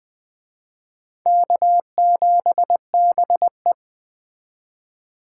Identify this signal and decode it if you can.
Morse code, heard as a keyed tone, about 21 wpm, 700 Hz. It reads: K7BE